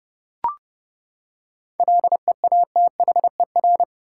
Morse code, heard as a keyed tone, about 30 wpm, 700 Hz. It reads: LEATHER
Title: Seven Letter Words - 30wpm